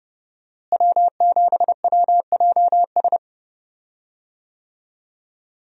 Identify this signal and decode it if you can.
Morse code, heard as a keyed tone, about 30 wpm, 700 Hz. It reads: W7WJS
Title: USA Call Signs - 5 Characters - Repeated 3x - 30wpm